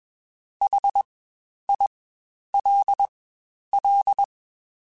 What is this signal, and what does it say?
Morse code, heard as a keyed tone, about 21 wpm, spaced as for 10 wpm, 800 Hz. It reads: HILL
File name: Spaced Alphas / Spaced Alphas mp3s / Hs sp @20